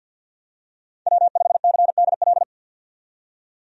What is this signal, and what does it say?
Morse code, heard as a keyed tone, about 50 wpm, 700 Hz. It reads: W5CDL
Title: USA Call Signs - 5 Characters - 50wpm